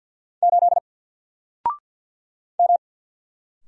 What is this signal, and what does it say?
Morse code, heard as a keyed tone, about 50 wpm, 700 Hz. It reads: 8 M
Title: Single Letter-Number 50wpm